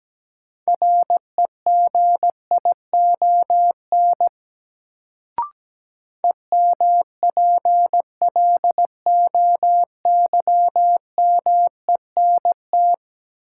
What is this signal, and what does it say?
Morse code, heard as a keyed tone, about 17 wpm, 700 Hz. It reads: REGION EMPLOYMENT
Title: Sentences from Top 1500 Words - New Word Drill Alt - 17wpm